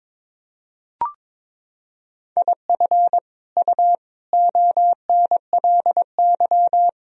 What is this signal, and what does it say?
Morse code, heard as a keyed tone, about 22 wpm, 700 Hz. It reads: IF U ONLY
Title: Sentences from Top 400 Words and Abbreviations 22wpm